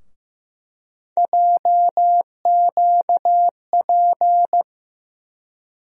Morse code, heard as a keyed tone, about 15 wpm, 700 Hz. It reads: JQP
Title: USA Call Sign Postfixes - 3 Letters 15wpm